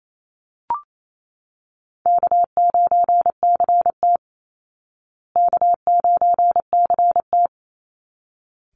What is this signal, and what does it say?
Morse code, heard as a keyed tone, about 28 wpm, 700 Hz. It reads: K9CT K9CT
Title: Call Signs Repeated 3x - 28wpm